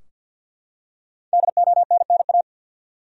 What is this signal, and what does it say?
Morse code, heard as a keyed tone, about 52 wpm, 700 Hz. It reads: DONNA
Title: USA Common Names - Max 5 Letters - 50wpm